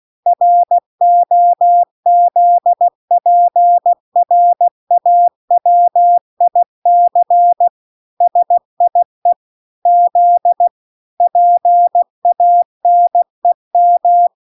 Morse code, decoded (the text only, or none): ROZPRAWIC SIE Z PANEM